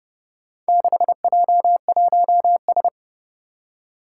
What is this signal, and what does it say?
Morse code, heard as a keyed tone, about 30 wpm, 700 Hz. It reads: 6J1S
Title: Four Challenging Letter-Number Combinations 30wpm